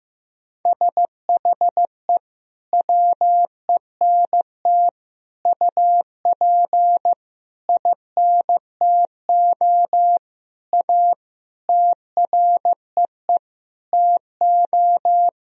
Morse code, heard as a keyed tone, about 15 wpm, 700 Hz. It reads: SHE WENT UP INTO A TREE TO
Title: Sentences from Top 1000 Words - Review Part 1 - 15wpm